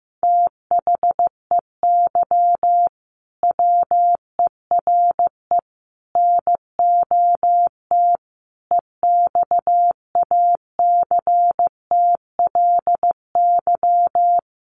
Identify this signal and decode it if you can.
Morse code, heard as a keyed tone, about 15 wpm, 700 Hz. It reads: THEY WERE NOT EXACTLY
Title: Sentences from Top 1000 Words 15wpm